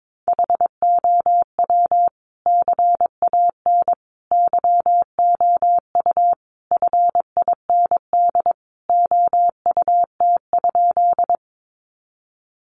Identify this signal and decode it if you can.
Morse code, heard as a keyed tone, about 22 wpm, 700 Hz. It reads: HOW CAN YOU FIND OUT?
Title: Sentences from Top 200 Words 22wpm